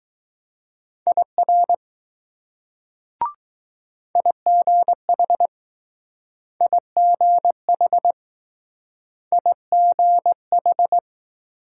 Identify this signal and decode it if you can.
Morse code, heard as a keyed tone, about 20 wpm, 700 Hz. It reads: IR IGH IGH IGH